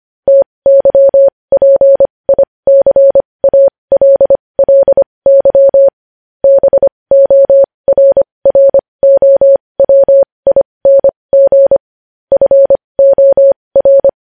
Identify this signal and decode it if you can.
Morse code, heard as a keyed tone, about 25 wpm, spaced as for 20 wpm, 550 Hz. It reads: TYPICALLY BORROWING FOR
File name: podcast / News headlines-1x-25x20